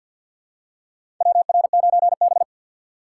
Morse code, heard as a keyed tone, about 50 wpm, 700 Hz. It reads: WR9B
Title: USA Call Signs - 4 Characters - 2x Character Spacing - 50wpm